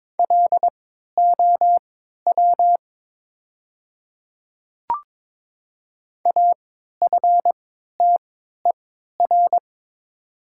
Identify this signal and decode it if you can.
Morse code, heard as a keyed tone, about 22 wpm, spaced as for 13 wpm, 700 Hz. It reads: LOW AFTER
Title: 3x Character Spacing - Top 200 Words - 22wpm